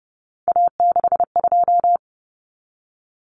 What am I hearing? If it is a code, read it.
Morse code, heard as a keyed tone, about 30 wpm, 700 Hz. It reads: A62
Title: Common Cut Numbers - 3-Digit 30wpm